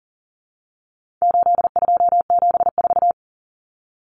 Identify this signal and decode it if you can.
Morse code, heard as a keyed tone, about 40 wpm, 700 Hz. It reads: 8274
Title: Numbers - 4-Digit 40wpm